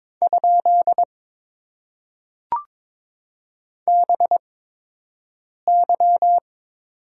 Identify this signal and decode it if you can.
Morse code, heard as a keyed tone, about 22 wpm, spaced as for 6 wpm, 700 Hz. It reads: ? BY
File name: Top 90 QSO Elements - 8x Character Spacing - 22wpm